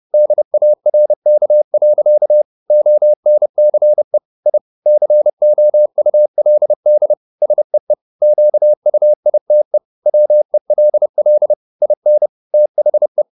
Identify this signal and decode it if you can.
Morse code, heard as a keyed tone, about 30 wpm, 600 Hz. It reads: DARK. ONCE I COULD SEE QUITE WELL IN THE
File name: the old man and the sea / the old man and the sea 1 30wpm